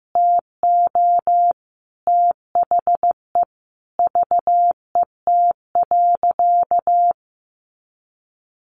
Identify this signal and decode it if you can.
Morse code, heard as a keyed tone, about 14 wpm, 700 Hz. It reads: TO THE VET.
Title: Easy Phrases 15wpm